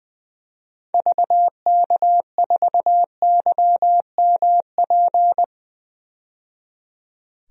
Morse code, heard as a keyed tone, about 20 wpm, 700 Hz. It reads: VK4YMP
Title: DXCC-100 Call Signs - 6 Characters - 20wpm